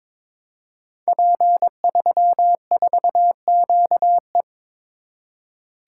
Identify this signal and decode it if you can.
Morse code, heard as a keyed tone, about 23 wpm, 700 Hz. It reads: P34QE
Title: Code Groups - 5 Letter:Numbers - 22wpm